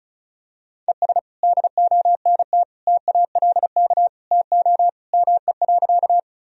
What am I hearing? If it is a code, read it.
Morse code, heard as a keyed tone, about 36 wpm, 700 Hz. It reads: ES DONT TALK TO ME.
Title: Sentences from Top 400 Words and Abbreviations 35wpm